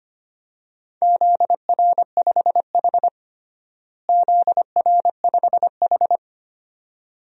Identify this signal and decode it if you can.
Morse code, heard as a keyed tone, about 25 wpm, 700 Hz. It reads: ZR5H ZR5H
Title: DXCC-100 Call Signs - 4 Characters - Repeated 2x - 25wpm